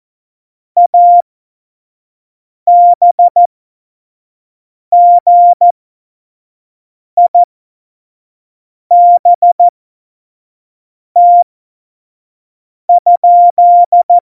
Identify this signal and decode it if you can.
Morse code, heard as a keyed tone, about 14 wpm, spaced as for 5 wpm, 700 Hz. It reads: ABGIBT?
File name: Morsekurs / Witze / Witz 21 April 2020, 25 F70